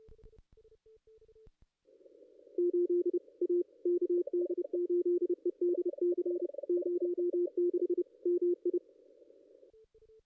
Morse code, heard as a keyed tone, about 30 wpm, 350 Hz. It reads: 8 A KD8EDC 06 MI
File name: contests / ss-cw-k5tr-op-n5rz -2200